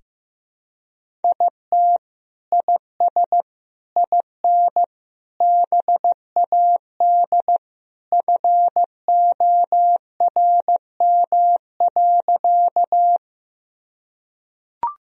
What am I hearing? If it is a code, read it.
Morse code, heard as a keyed tone, about 15 wpm, 700 Hz. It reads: IT IS IN BAD FORM.
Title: Sentences from Top 1000 Words - Review Part 1 - 15wpm